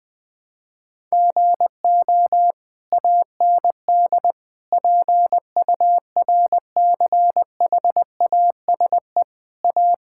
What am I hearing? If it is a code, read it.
Morse code, heard as a keyed tone, about 20 wpm, 700 Hz. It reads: GO AND PURCHASE A
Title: Sentences from Top 1000 Words - Review Part 5 - 20wpm